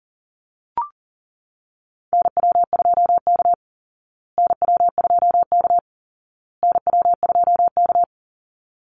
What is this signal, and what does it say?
Morse code, heard as a keyed tone, about 40 wpm, 700 Hz. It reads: NW2K NW2K NW2K
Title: Call Signs Repeated 2x - 40wpm